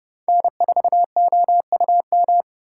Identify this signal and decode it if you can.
Morse code, heard as a keyed tone, about 31 wpm, 700 Hz. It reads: N4OUM
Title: USA Call Signs - 5 Characters - 3x Character Spacing - 30wpm